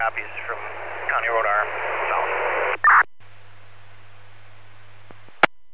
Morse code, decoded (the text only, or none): none